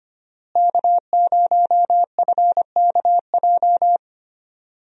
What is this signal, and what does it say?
Morse code, heard as a keyed tone, about 25 wpm, 690 Hz. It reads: K0FKJ
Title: USA Call Signs - 5 Characters - Repeated 2x - 25wpm